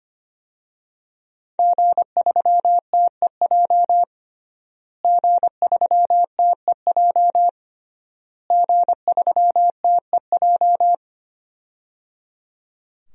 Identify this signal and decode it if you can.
Morse code, heard as a keyed tone, about 25 wpm, 700 Hz. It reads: G3TEJ G3TEJ G3TEJ
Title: DXCC-100 Call Signs - 5 Characters - Repeated 3x - 25wpm